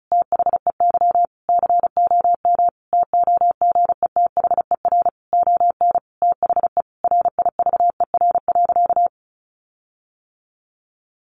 Morse code, heard as a keyed tone, about 34 wpm, 700 Hz. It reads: THEY COM TOGETHER ON THE RIVER.
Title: Sentences from Top 1300 Words 35wpm